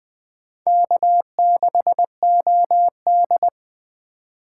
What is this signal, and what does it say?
Morse code, heard as a keyed tone, about 21 wpm, 700 Hz. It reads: K6OD
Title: Speed Racing - USA 1x2 Call Signs 17wpm